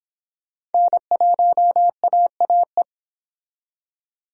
Morse code, heard as a keyed tone, about 30 wpm, 700 Hz. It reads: N1AAE